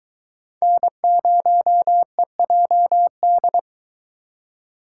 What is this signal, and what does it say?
Morse code, heard as a keyed tone, about 23 wpm, 700 Hz. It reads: N0EJD